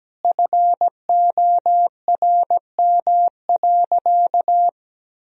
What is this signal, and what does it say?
Morse code, heard as a keyed tone, about 17 wpm, 700 Hz. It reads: FORM.
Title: Sentences from Top 1000 Words - Review Part 1 - 17wpm